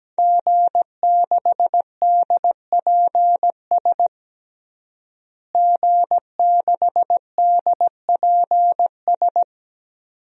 Morse code, heard as a keyed tone, about 17 wpm, 700 Hz. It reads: G6DPS G6DPS
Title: DXCC-100 Call Signs - 5 Characters - Repeated 2x - 17wpm